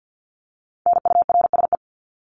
Morse code, heard as a keyed tone, about 48 wpm, 700 Hz. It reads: NURSE
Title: Top 1751-2000 Words 50wpm